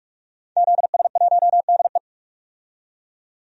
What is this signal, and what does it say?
Morse code, heard as a keyed tone, about 46 wpm, 700 Hz. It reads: ZS1DE
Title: DXCC-100 Call Signs - 5 Characters - 45wpm